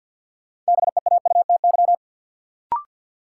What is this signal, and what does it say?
Morse code, heard as a keyed tone, about 49 wpm, 700 Hz. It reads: BEAUTY
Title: Sentences from Top 600 Words - New Word Drill - 50wpm